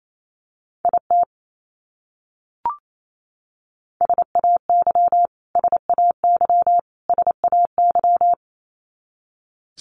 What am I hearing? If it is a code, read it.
Morse code, heard as a keyed tone, about 28 wpm, 700 Hz. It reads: IT SAY SAY SAY
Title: Top 300 Words Repeated 28wpm